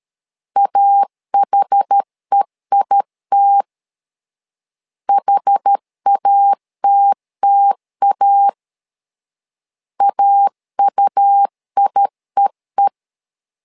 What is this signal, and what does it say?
Morse code, heard as a keyed tone, about 13 wpm, 800 Hz. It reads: AHEIT HATTA AUIEE
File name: LEZIONI / 2 UHDN / 2 UH 12wpm 1439